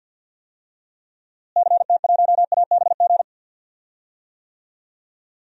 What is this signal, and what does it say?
Morse code, heard as a keyed tone, about 48 wpm, 700 Hz. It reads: CT1ABG
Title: DXCC-100 Call Signs - 6 Characters - Repeated 2x - 50wpm